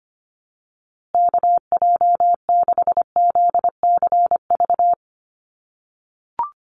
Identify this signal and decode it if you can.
Morse code, heard as a keyed tone, about 25 wpm, 700 Hz. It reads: KJ6ZCV
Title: USA Call Signs Repeated 3x 25wpm